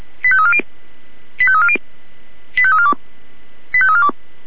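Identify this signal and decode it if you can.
Five-tone selective call (ZVEI), heard as quick runs of five tones, tones of about 70 ms. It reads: 85340 85340 85342 85342
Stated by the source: ZVEI five-tone selective call received by SDR.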